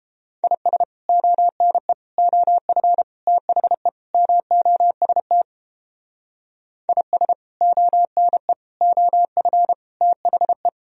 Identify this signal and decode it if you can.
Morse code, heard as a keyed tone, about 29 wpm, 700 Hz. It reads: IS ONE OF THE MOST IS ONE OF THE